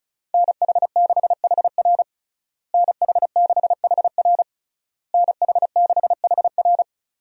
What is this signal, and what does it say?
Morse code, heard as a keyed tone, about 34 wpm, 700 Hz. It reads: NH6HR NH6HR NH6HR
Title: USA Call Signs - 5 Characters - Repeated 3x - 35wpm